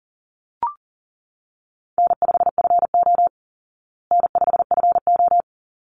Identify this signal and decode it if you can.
Morse code, heard as a keyed tone, about 40 wpm, 700 Hz. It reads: N5FO N5FO
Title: Easy Call Signs Repeated 2x - 40wpm